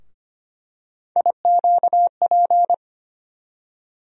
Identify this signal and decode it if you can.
Morse code, heard as a keyed tone, about 25 wpm, 700 Hz. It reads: IQP
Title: USA Call Sign Postfixes - 3 Letters 25wpm